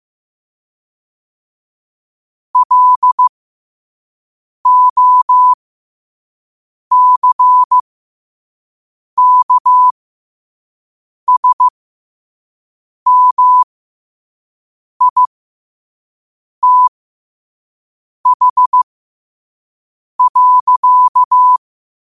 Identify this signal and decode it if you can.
Morse code, heard as a keyed tone, about 15 wpm, spaced as for 5 wpm, 1000 Hz. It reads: LOCKSMITH.
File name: qotdfiles / 5wpm